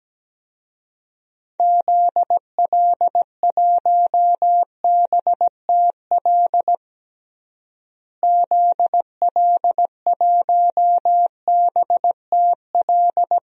Morse code, heard as a keyed tone, about 17 wpm, 700 Hz. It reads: ZL1BTL ZL1BTL
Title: DXCC-100 Call Signs - 6 Characters - Repeated 3x - 17wpm